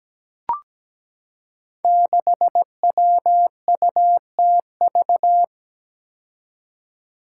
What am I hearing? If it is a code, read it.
Morse code, heard as a keyed tone, about 17 wpm, 700 Hz. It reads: 6WUTV
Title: Five Challenging Letter-Number Combinations 17wpm